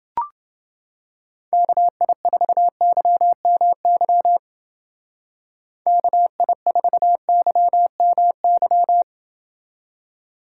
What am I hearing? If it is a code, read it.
Morse code, heard as a keyed tone, about 27 wpm, 700 Hz. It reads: KI4YMY KI4YMY